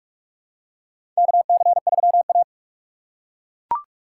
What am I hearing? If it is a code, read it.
Morse code, heard as a keyed tone, about 42 wpm, 700 Hz. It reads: KC3A
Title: USA Call Signs - 4 Characters - 3x Character Spacing - 45wpm